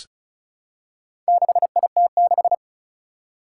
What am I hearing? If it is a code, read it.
Morse code, heard as a keyed tone, about 36 wpm, 700 Hz. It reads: 6IT6
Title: Four Challenging Letter-Number Combinations 35wpm